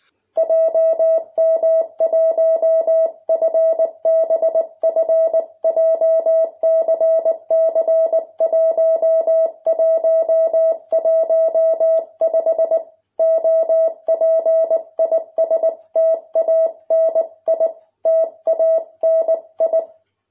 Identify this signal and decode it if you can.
Morse code, heard as a keyed tone, about 19 wpm, 650 Hz. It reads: JM1FBF JCC1115 OP IS TANI TANI